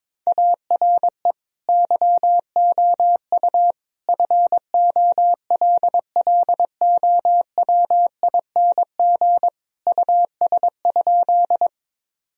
Morse code, heard as a keyed tone, about 22 wpm, 700 Hz. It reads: ARE YOU FOLLOWING US?
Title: Sentences from Top 1000 Words - Review Part 4 - 22wpm